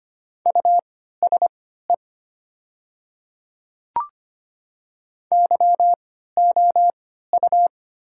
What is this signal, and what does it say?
Morse code, heard as a keyed tone, about 25 wpm, spaced as for 14 wpm, 690 Hz. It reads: USE YOU
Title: 3x Character Spacing - Top 100 Words - 25wpm